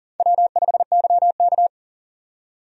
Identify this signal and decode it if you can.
Morse code, heard as a keyed tone, about 40 wpm, 700 Hz. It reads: W5YK
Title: USA Call Signs - 4 Characters - 40wpm